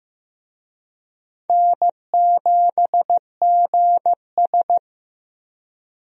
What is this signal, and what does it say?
Morse code, heard as a keyed tone, about 15 wpm, 700 Hz. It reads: N7GS
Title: USA Call Signs - 4 Characters - Repeated 2x - 15wpm